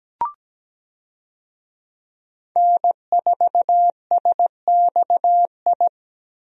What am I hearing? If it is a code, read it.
Morse code, heard as a keyed tone, about 17 wpm, 700 Hz. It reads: N4SXI
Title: USA Call Signs - 5 Characters - Repeated 3x - 17wpm